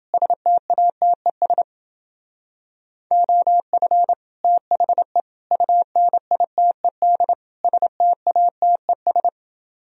Morse code, heard as a keyed tone, about 27 wpm, 700 Hz. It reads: STATES OF THE UNITED STATES